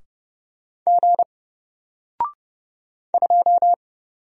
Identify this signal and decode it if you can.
Morse code, heard as a keyed tone, about 30 wpm, 700 Hz. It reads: G 2
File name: Single-Letter-Number-30wpm